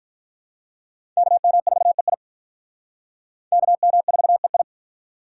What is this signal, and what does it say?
Morse code, heard as a keyed tone, about 44 wpm, 700 Hz. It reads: KM4EI KM4EI